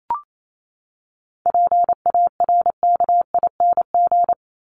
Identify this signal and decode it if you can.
Morse code, heard as a keyed tone, about 28 wpm, 700 Hz. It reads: PARKING
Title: Top 1501-1750 Words 28wpm